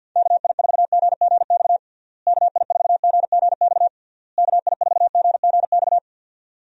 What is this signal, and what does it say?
Morse code, heard as a keyed tone, about 52 wpm, 700 Hz. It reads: KI4GGX KI4GGX KI4GGX
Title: USA Call Signs - 6 Characters - Repeated 3x - 50wpm